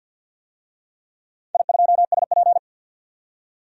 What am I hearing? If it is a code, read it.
Morse code, heard as a keyed tone, about 47 wpm, 700 Hz. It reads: I2SP
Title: DXCC-100 Call Signs - 4 Characters - 50wpm